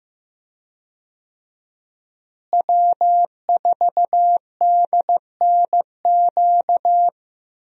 Morse code, heard as a keyed tone, about 16 wpm, 700 Hz. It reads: W4DNQ
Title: USA Call Signs - 5 Characters - Repeated 3x - 15wpm